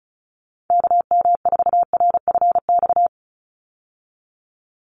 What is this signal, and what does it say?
Morse code, heard as a keyed tone, about 34 wpm, 700 Hz. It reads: KM4RFX
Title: USA Call Signs 35wpm